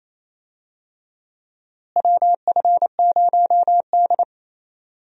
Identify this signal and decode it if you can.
Morse code, heard as a keyed tone, about 28 wpm, 700 Hz. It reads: WF0D
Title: USA Call Signs - 4 Characters - Repeated 3x - 28wpm